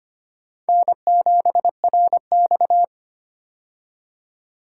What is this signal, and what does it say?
Morse code, heard as a keyed tone, about 25 wpm, 700 Hz. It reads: N7RX
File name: practice / 25wpm / Four Letter-Number Combinations - 25wpm